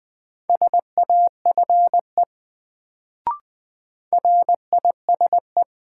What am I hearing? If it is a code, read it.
Morse code, heard as a keyed tone, about 20 wpm, 700 Hz. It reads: SAFE RISE
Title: Sentences from Top 900 Words - New Word Drill - 20wpm